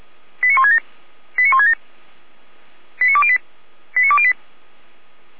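Five-tone selective call (ZVEI), heard as quick runs of five tones, tones of about 70 ms. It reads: 89167 89167 89208 89208
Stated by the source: ZVEI five-tone call received by SDR.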